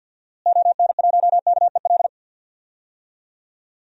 Morse code, heard as a keyed tone, about 47 wpm, 700 Hz. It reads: ON1KEL
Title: DXCC-100 Call Signs - 6 Characters - Repeated 3x - 50wpm